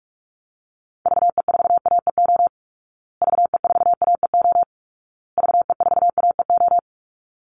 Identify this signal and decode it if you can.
Morse code, heard as a keyed tone, about 44 wpm, 700 Hz. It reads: VE4AEO VE4AEO VE4AEO
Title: Call Signs Repeated 2x - 45wpm